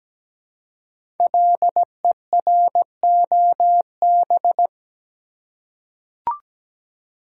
Morse code, heard as a keyed tone, about 17 wpm, 700 Hz. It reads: LEROB
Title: Code Groups - 5 Letters - 17wpm